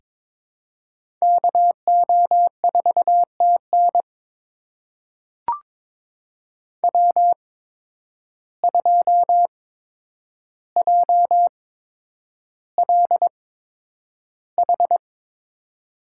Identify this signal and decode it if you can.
Morse code, heard as a keyed tone, about 22 wpm, spaced as for 9 wpm, 700 Hz. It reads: KO4TN W2JLH
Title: USA Call Signs - 8x Character Spacing - 22wpm